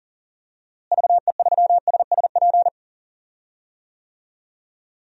Morse code, heard as a keyed tone, about 40 wpm, 700 Hz. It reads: VE3SSP
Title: DXCC-100 Call Signs - 6 Characters - Repeated 3x - 40wpm